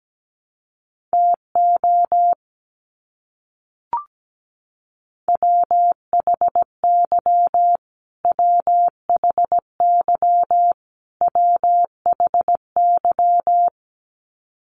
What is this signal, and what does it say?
Morse code, heard as a keyed tone, about 17 wpm, 700 Hz. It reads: TO WHY WHY WHY
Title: Top 500 Words Repeated 17wpm